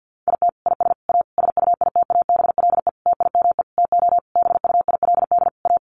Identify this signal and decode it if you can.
Morse code, heard as a keyed tone, about 50 wpm, 700 Hz. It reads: IT IS A SUITABLE TIME TO BUILD A
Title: Sentences from Top 1400 Words 50wpm